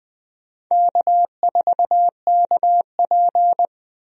Morse code, heard as a keyed tone, about 20 wpm, 700 Hz. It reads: K4KP
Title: USA Call Signs - 4 Characters - 20wpm